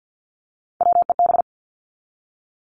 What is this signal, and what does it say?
Morse code, heard as a keyed tone, about 50 wpm, 700 Hz. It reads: WEB
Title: Top 501-750 Words 50wpm